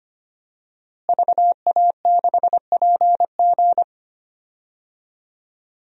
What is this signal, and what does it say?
Morse code, heard as a keyed tone, about 25 wpm, 700 Hz. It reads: VA6PG
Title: DXCC-100 Call Signs - 5 Characters - Repeated 2x - 25wpm